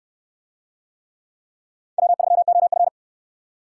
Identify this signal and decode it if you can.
Morse code, heard as a keyed tone, about 63 wpm, 700 Hz. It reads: W3OF